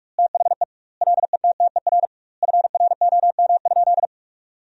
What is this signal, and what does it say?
Morse code, heard as a keyed tone, about 47 wpm, 700 Hz. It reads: THE LETTER FROM?
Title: 2x Word-Spacing - Sentences from Top 300 Words - 45wpm